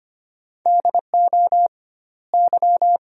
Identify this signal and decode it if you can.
Morse code, heard as a keyed tone, about 25 wpm, 700 Hz. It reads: DO Y
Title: 2x Word-Spacing - Sentences from Top 500 Words - 25wpm